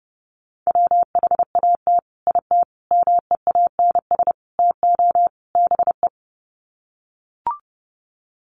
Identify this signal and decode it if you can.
Morse code, heard as a keyed tone, about 30 wpm, 700 Hz. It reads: WHAT IT MEANS TO BE
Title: Sets of 5 Words 30wpm